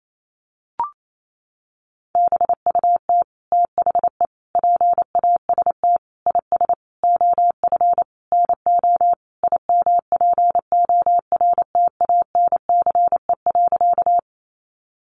Morse code, heard as a keyed tone, about 28 wpm, 700 Hz. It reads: BUT THE PAST IS OF NO IMPORTANCE.
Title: Sentences from Top 1200 Words 28wpm